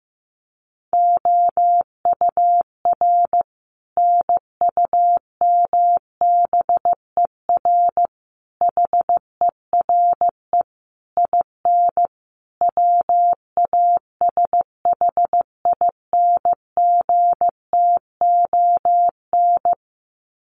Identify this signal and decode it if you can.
Morse code, heard as a keyed tone, about 15 wpm, 700 Hz. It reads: OUR NUMBER HERE IN WASHINGTON
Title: Sets of 5 Words Encore2 15wpm